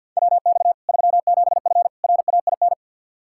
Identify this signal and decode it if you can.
Morse code, heard as a keyed tone, about 50 wpm, 700 Hz. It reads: WX 37F RAIN
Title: Practice QSO - Exchange 2 - Info - 50wpm